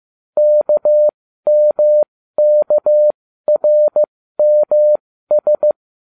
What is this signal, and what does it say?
Morse code, heard as a keyed tone, about 15 wpm, spaced as for 13 wpm, 600 Hz. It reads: KMKRMS